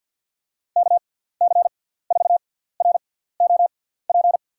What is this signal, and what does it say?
Morse code, heard as a keyed tone, about 51 wpm, spaced as for 17 wpm, 700 Hz. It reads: KC4ROP
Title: USA Call Signs - 6 Characters - 6x Character Spacing - 50wpm